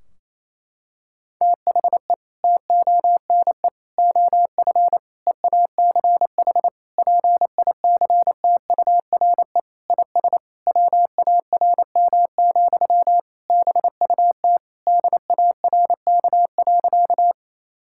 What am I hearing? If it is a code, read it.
Morse code, heard as a keyed tone, about 28 wpm, 700 Hz. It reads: THE TONE OF EACH PICTURE IS WARM, BUT DARK.
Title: Sentences from Top 1600 Words 28wpm